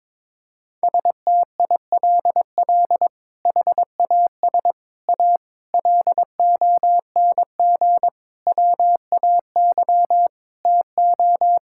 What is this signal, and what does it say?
Morse code, heard as a keyed tone, about 22 wpm, 700 Hz. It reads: STILL HAS A LONG WAY TO
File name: practice / 22wpm / Sets of 7 Words - 22wpm